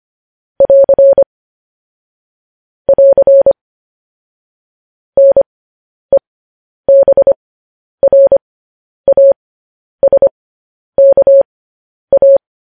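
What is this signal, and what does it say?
Morse code, heard as a keyed tone, about 25 wpm, spaced as for 10 wpm, 550 Hz. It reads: <AR> <AR> NEBRASKA
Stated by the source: Morse transmission of news headlines.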